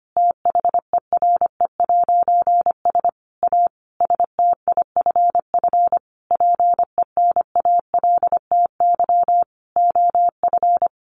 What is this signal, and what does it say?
Morse code, heard as a keyed tone, about 25 wpm, 700 Hz. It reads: THERE'S A STIFF PENALTY OF
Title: A Sound Of Thunder 25wpm